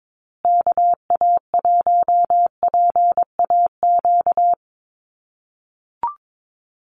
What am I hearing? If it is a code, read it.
Morse code, heard as a keyed tone, about 22 wpm, 700 Hz. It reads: KA1PAQ
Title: USA Call Signs Repeated 3x 22wpm